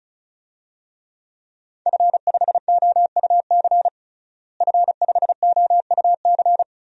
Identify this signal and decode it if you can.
Morse code, heard as a keyed tone, about 36 wpm, 700 Hz. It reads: F5OUC F5OUC
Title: DXCC-100 Call Signs - 5 Characters - Repeated 2x - 35wpm